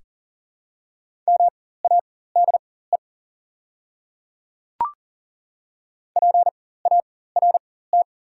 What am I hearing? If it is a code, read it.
Morse code, heard as a keyed tone, about 40 wpm, spaced as for 19 wpm, 690 Hz. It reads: MADE PART
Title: 4x Character Spacing - Top 200 Words - 40wpm